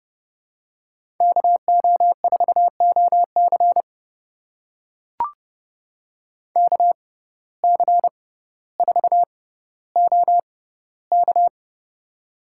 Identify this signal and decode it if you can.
Morse code, heard as a keyed tone, about 32 wpm, spaced as for 16 wpm, 700 Hz. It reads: KO4OC KC4OK
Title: USA Call Signs - 5 Characters - 6x Character Spacing - 30wpm